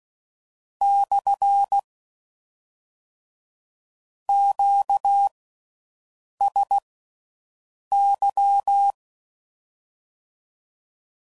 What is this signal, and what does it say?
Morse code, heard as a keyed tone, about 16 wpm, spaced as for 6 wpm, 800 Hz. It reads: / QSY